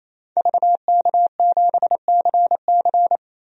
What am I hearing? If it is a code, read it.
Morse code, heard as a keyed tone, about 28 wpm, 700 Hz. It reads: VK7CC
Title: DXCC-100 Call Signs - 5 Characters - 28wpm